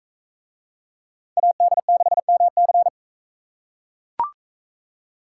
Morse code, heard as a keyed tone, about 40 wpm, 700 Hz. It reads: AD6MC